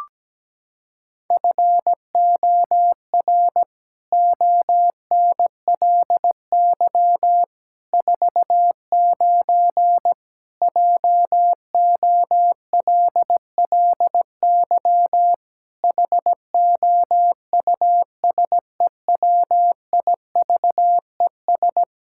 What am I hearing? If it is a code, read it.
Morse code, heard as a keyed tone, about 17 wpm, 700 Hz. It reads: FOR ONLY 49 JOLLY HOUSEWIVES
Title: Pangram Challenge v2 - 17wpm